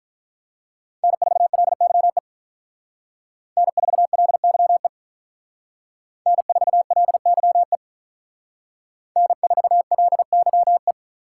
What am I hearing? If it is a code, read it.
Morse code, heard as a keyed tone, about 45 wpm, 700 Hz. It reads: N4LYE N4LYE N4LYE N4LYE